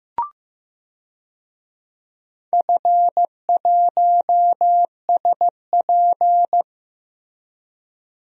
Morse code, heard as a keyed tone, about 16 wpm, 700 Hz. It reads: F1SP
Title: DXCC-100 Call Signs - 4 Characters - Repeated 2x - 15wpm